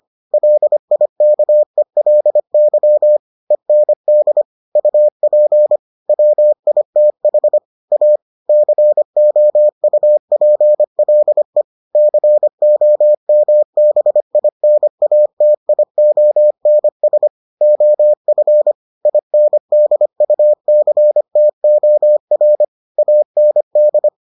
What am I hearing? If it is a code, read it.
Morse code, heard as a keyed tone, about 25 wpm, 600 Hz. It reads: LIKELY END UP WITH A COUPLE COMBINATIONS OF INDUCTOR AND